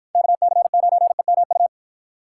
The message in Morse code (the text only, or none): KC9EMU